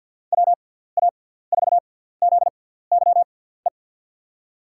Morse code, heard as a keyed tone, about 52 wpm, spaced as for 17 wpm, 700 Hz. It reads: WA4ZYE
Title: USA Call Signs - 6 Characters - 6x Character Spacing - 50wpm